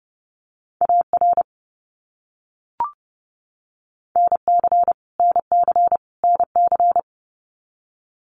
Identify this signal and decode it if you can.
Morse code, heard as a keyed tone, about 30 wpm, 700 Hz. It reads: AR NC NC NC
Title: US States Repeated 3x 30wpm